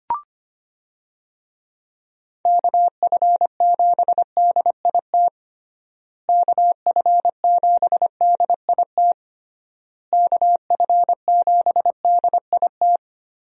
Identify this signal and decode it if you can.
Morse code, heard as a keyed tone, about 25 wpm, 700 Hz. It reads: KF7DIT KF7DIT KF7DIT
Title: USA Call Signs - 6 Characters - Repeated 3x - 25wpm